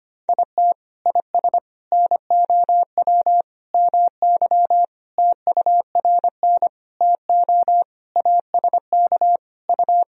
Morse code, heard as a keyed tone, about 25 wpm, 700 Hz. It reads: IT IS NOW MY TURN TO ASK U
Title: Sentences from Top 200 Words and Abbreviations 25wpm